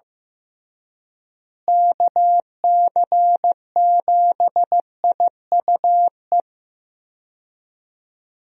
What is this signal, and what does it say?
Morse code, heard as a keyed tone, about 15 wpm, 700 Hz. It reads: KC7IUE
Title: USA Call Signs - 6 Characters - Repeated 3x - 15wpm